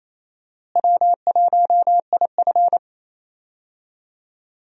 Morse code, heard as a keyed tone, about 28 wpm, 700 Hz. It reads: W1IF
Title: USA Call Signs - 4 Characters - Repeated 3x - 28wpm